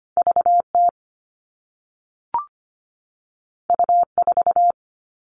Morse code, heard as a keyed tone, about 25 wpm, 700 Hz. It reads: V T U 4
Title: All Cut Numbers - 2-Digit 25wpm